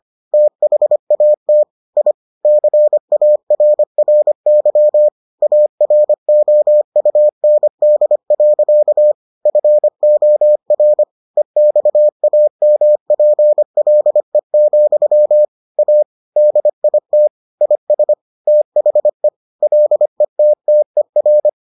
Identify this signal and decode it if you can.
Morse code, heard as a keyed tone, about 25 wpm, 600 Hz. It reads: THAT I CARRY AROUND. FOR EXAMPLE, A DIT IS THE LETTER